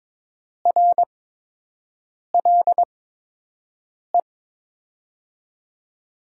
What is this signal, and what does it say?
Morse code, heard as a keyed tone, about 21 wpm, spaced as for 6 wpm, 700 Hz. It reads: RLE
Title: USA Call Signs - 6 Characters - 8x Character Spacing - 22wpm